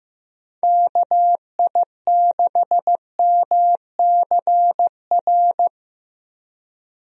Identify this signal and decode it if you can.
Morse code, heard as a keyed tone, about 15 wpm, 700 Hz. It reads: KI6MCR
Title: USA Call Signs - 6 Characters - Repeated 2x - 15wpm